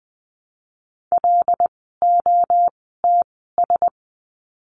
Morse code, heard as a keyed tone, about 20 wpm, 700 Hz. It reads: L O T S
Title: 075 Lesson 13f - Letter L - Words 20wpm